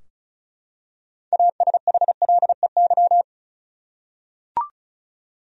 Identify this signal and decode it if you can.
Morse code, heard as a keyed tone, about 36 wpm, 700 Hz. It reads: ASHLEY
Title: USA Common Names - 35wpm